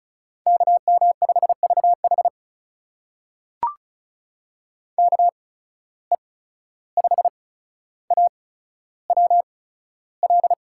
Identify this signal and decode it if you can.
Morse code, heard as a keyed tone, about 36 wpm, spaced as for 14 wpm, 700 Hz. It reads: KM5VH KE5AWL
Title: USA Call Signs - 8x Character Spacing - 35wpm